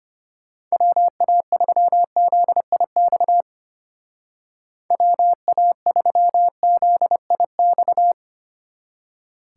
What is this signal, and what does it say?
Morse code, heard as a keyed tone, about 31 wpm, 700 Hz. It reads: WA3ZIX WA3ZIX
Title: Speed Racing - USA Call Signs 25wpm